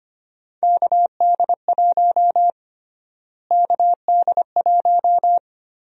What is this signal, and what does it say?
Morse code, heard as a keyed tone, about 25 wpm, 700 Hz. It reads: KD1 KD1
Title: USA Call Sign Prefixes - Repeated 2x - 25wpm